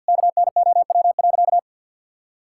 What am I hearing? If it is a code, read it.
Morse code, heard as a keyed tone, about 50 wpm, 700 Hz. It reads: KNOW.
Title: Sentences from Top 1000 Words - Review Part 5 - 50wpm